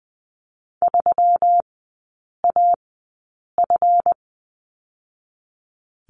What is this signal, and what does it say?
Morse code, heard as a keyed tone, about 20 wpm, spaced as for 8 wpm, 700 Hz. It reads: 3AF
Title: 138 Lesson 23e - Letter f - Three Characters 20wpm